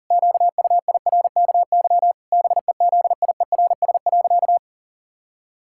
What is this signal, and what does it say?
Morse code, heard as a keyed tone, about 40 wpm, 700 Hz. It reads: QUIRKY BEZIERS.
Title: Pangram Challenge v2 - 40wpm